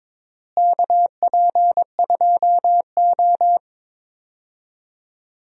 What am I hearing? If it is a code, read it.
Morse code, heard as a keyed tone, about 22 wpm, 700 Hz. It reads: KP2O
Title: DXCC-100 Call Signs - 4 Characters - Repeated 3x - 22wpm